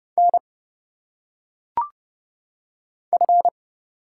Morse code, heard as a keyed tone, about 30 wpm, 700 Hz. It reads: N F
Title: Single Letters - Phonetic Pronunciation 30wpm